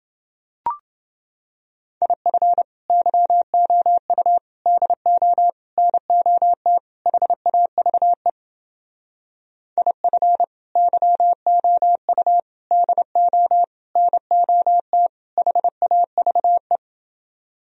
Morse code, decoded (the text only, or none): IF YOU DO NOT HAVE IF YOU DO NOT HAVE